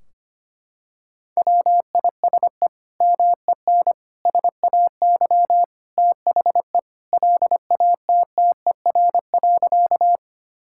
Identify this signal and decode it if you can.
Morse code, heard as a keyed tone, about 25 wpm, 700 Hz. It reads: WISE MEN SAY THE LATTER.
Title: Sentences from Top 1600 Words 25wpm